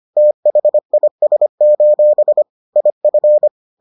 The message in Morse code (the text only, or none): THIS: IF